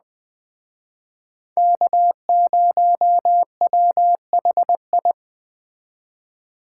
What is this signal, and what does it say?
Morse code, heard as a keyed tone, about 20 wpm, 700 Hz. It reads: K0WHI
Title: Speed Racing - USA 1x3 Call Signs 17wpm